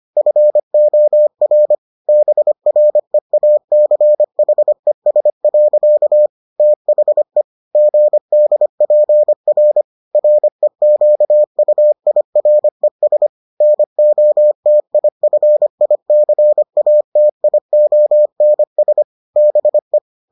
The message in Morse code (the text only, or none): FOR BREACHES. THE GDPR REQUIRES NOTIFICATIONS BE